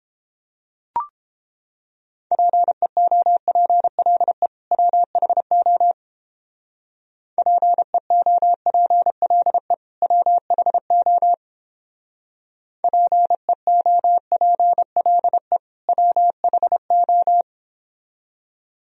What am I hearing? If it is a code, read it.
Morse code, heard as a keyed tone, about 29 wpm, 700 Hz. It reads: PEOPLE WHO PEOPLE WHO PEOPLE WHO